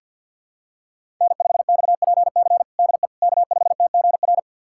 Morse code, heard as a keyed tone, about 51 wpm, 700 Hz. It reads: N5XPC DE K5TGR
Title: USA Callsign DE Callsign - 5 Characters - 50wpm